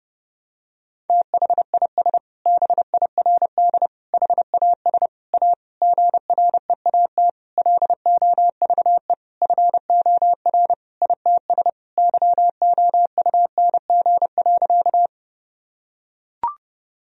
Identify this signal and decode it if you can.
Morse code, heard as a keyed tone, about 30 wpm, 700 Hz. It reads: THIS BIRD HAS A GREAT LOVE FOR ITS YOUNG.
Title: Sentences from Top 1000 Words - Review Part 2 - 30wpm